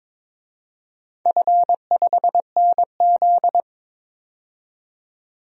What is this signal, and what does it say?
Morse code, heard as a keyed tone, about 22 wpm, 700 Hz. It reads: F5NZ
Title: DXCC-100 Call Signs - 4 Characters - 22wpm